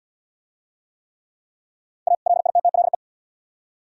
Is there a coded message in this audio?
Yes: Morse code from 2.1 to 3.0 s, heard as a keyed tone, about 68 wpm, 700 Hz. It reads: A LITTLE